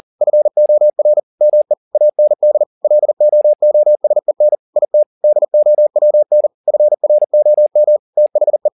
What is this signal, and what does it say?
Morse code, heard as a keyed tone, about 40 wpm, 600 Hz. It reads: FOR ME AND LOOSEN IT DOWN FROM THE